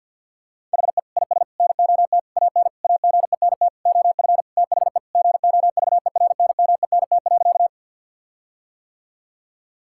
Morse code, heard as a keyed tone, about 50 wpm, 700 Hz. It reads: HE IS NOT AN AGENT OF THE GOVERNMENT.
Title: Sentences from Top 1600 Words 50wpm